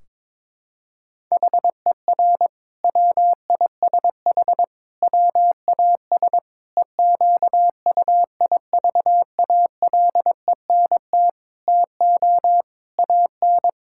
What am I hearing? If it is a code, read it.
Morse code, heard as a keyed tone, about 22 wpm, 700 Hz. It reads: HER WISH WAS EQUIVALENT TO AN
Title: Sentences from Top 1600 Words 22wpm